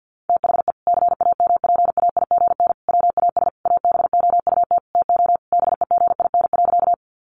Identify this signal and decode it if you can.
Morse code, heard as a keyed tone, about 50 wpm, 700 Hz. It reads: THE CAMPAIGN WAS ABOUT TO BEGIN.
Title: Sentences from Top 1200 Words 50wpm